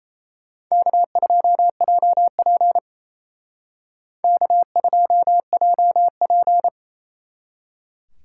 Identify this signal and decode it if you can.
Morse code, heard as a keyed tone, about 34 wpm, 700 Hz. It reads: K2JP K2JP
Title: Speed Racing - USA 1x2 Call Signs 28wpm